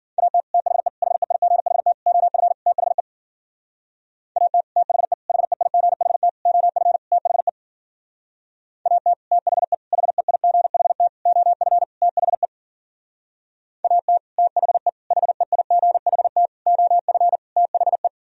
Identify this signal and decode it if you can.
Morse code, heard as a keyed tone, about 51 wpm, 700 Hz. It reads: AT THE HEIGHT OF THE AT THE HEIGHT OF THE AT THE HEIGHT OF THE AT THE HEIGHT OF THE